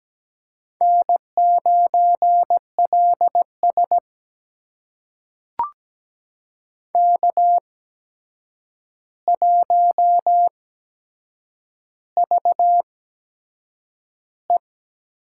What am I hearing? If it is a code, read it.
Morse code, heard as a keyed tone, about 17 wpm, spaced as for 7 wpm, 700 Hz. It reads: N9LS K1VE
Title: USA Call Signs - 4 Characters - 8x Character Spacing - 17wpm